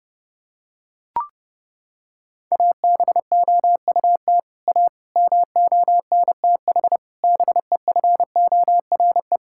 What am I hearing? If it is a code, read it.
Morse code, heard as a keyed tone, about 30 wpm, 700 Hz. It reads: ABOUT A MONTH BEFORE
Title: Sentences from Top 1000 Words - Review Part 5 - 30wpm